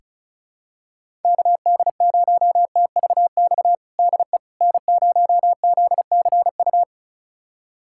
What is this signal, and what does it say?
Morse code, heard as a keyed tone, about 36 wpm, 700 Hz. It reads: KD0TVX DE N0ZCU
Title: USA Callsign DE Callsign 35wpm